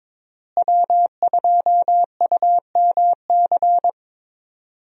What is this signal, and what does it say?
Morse code, heard as a keyed tone, about 21 wpm, 700 Hz. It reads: W2UMC
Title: USA Call Signs - 5 Characters - Repeated 3x - 22wpm